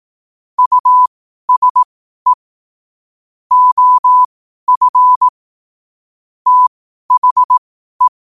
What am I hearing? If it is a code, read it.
Morse code, heard as a keyed tone, about 18 wpm, spaced as for 12 wpm, 1000 Hz. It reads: USE OF THE
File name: qotdfiles / 12wpm